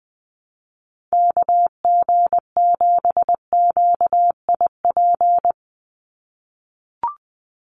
Morse code, heard as a keyed tone, about 20 wpm, 700 Hz. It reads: KG7QIP
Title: USA Call Signs 20wpm